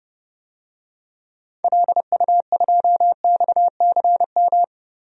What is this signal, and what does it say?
Morse code, heard as a keyed tone, about 30 wpm, 700 Hz. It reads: LU2XCM
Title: DXCC-100 Call Signs - 6 Characters - 30wpm